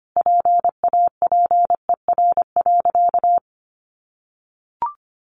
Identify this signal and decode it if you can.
Morse code, heard as a keyed tone, about 25 wpm, 700 Hz. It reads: PAPER.
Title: Sentences from Top 300 Words 25wpm